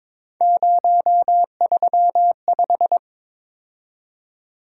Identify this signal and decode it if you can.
Morse code, heard as a keyed tone, about 22 wpm, 700 Hz. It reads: 035
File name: practice / 22wpm / Three Letter-Number Combinations - 22wpm